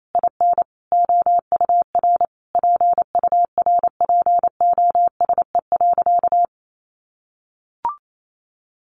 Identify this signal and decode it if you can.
Morse code, heard as a keyed tone, about 28 wpm, 700 Hz. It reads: IN OUR PURPOSE.
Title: Sentences from Top 800 Words 28wpm